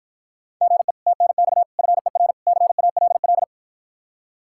Morse code, heard as a keyed tone, about 52 wpm, 700 Hz. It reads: GE TNX FER CALL